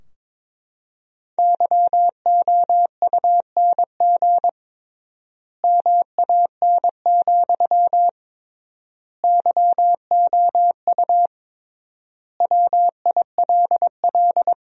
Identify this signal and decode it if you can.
Morse code, heard as a keyed tone, about 22 wpm, 700 Hz. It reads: YOUNG MAN, YOU WILL
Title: 3x Word-Spacing - Sentences from Top 400 Words - 22wpm